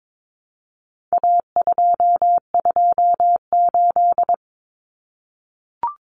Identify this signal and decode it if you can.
Morse code, heard as a keyed tone, about 22 wpm, 700 Hz. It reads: A228
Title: Common Cut Numbers - 4-Digit 22wpm